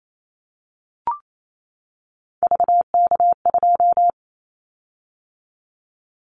Morse code, heard as a keyed tone, about 28 wpm, 700 Hz. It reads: VK2
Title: Callsign Prefixes for Top 100 DXCC - 28wpm